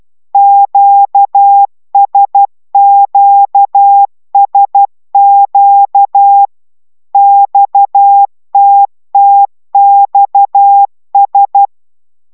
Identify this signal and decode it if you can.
Morse code, heard as a keyed tone, about 12 wpm, 800 Hz. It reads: QSQSQ XTTXS
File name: taviro tanulas / cw 69 16 10